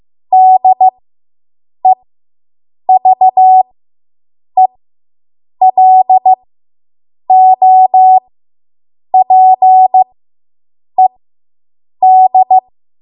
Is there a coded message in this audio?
Yes: Morse code from 0.3 to 12.6 s, heard as a keyed tone, about 15 wpm, spaced as for 7 wpm, 750 Hz. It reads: DEVELOPED